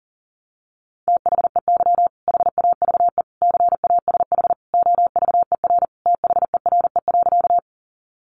Morse code, heard as a keyed tone, about 40 wpm, 700 Hz. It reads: THEY HAVE CASH OVER THERE.
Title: Sentences from Top 700 Words 40wpm